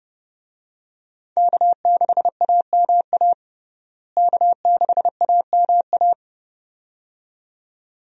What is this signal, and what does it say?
Morse code, heard as a keyed tone, about 29 wpm, 700 Hz. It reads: K6AMA K6AMA
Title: USA Call Signs - 5 Characters - Repeated 2x - 30wpm